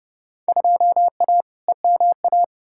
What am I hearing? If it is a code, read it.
Morse code, heard as a keyed tone, about 30 wpm, 700 Hz. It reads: 2A EMA
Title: Field Day Exchanges - 30wpm